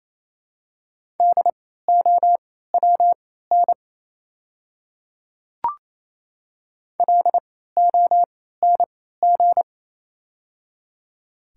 Morse code, heard as a keyed tone, about 29 wpm, spaced as for 16 wpm, 700 Hz. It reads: DOWN LONG
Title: 3x Character Spacing - Top 100 Words - 28wpm